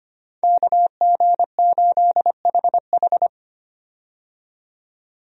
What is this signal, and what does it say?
Morse code, heard as a keyed tone, about 25 wpm, 700 Hz. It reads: KG8HH
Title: USA Call Signs - 5 Characters - 25wpm